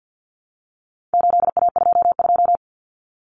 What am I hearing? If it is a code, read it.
Morse code, heard as a keyed tone, about 50 wpm, 700 Hz. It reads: 8A22
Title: Common Cut Numbers - 4-Digit 50wpm